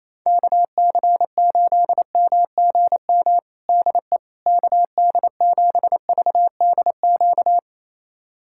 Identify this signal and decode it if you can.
Morse code, heard as a keyed tone, about 28 wpm, 700 Hz. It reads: KC8MGM DE KD7VDQ
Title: USA Callsign DE Callsign - 6 Characters - 28wpm